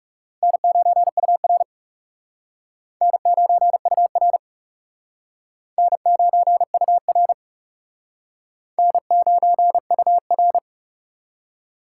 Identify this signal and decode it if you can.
Morse code, heard as a keyed tone, about 36 wpm, 700 Hz. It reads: N9UR N9UR N9UR N9UR
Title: Speed Racing - USA 1x2 Call Signs 30wpm